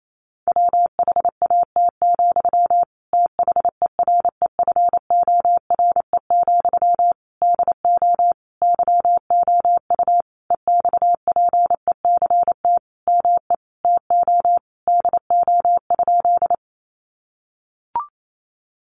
Morse code, heard as a keyed tone, about 28 wpm, 700 Hz. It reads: WHAT, THEREFORE, DO YOU EXPECT ME TO DO?
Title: Sentences from Top 1100 Words 28wpm